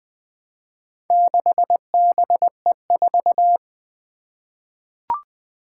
Morse code, heard as a keyed tone, about 21 wpm, 700 Hz. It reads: 6BE4
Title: Four Challenging Letter-Number Combinations 20wpm